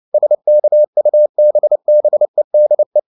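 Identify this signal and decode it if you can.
Morse code, heard as a keyed tone, about 29 wpm, 600 Hz. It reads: SKUBBEDE
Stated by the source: Fast (QRQ) Morse at 29 wpm, from CW med Gnister.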